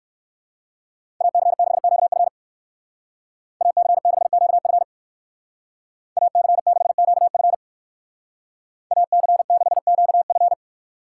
Morse code, heard as a keyed tone, about 56 wpm, 700 Hz. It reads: AC6QF AC6QF AC6QF AC6QF
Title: Speed Racing - USA 2x2 Call Signs 45wpm